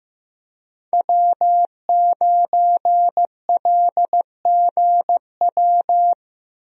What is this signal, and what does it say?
Morse code, heard as a keyed tone, about 15 wpm, 700 Hz. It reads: W9LGW
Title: USA Call Signs - 5 Characters - Repeated 2x - 15wpm